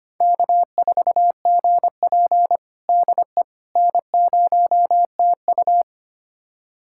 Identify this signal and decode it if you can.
Morse code, heard as a keyed tone, about 25 wpm, 700 Hz. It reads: K4GP DE N0TU
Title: USA Callsign DE Callsign - 4 Characters - 25wpm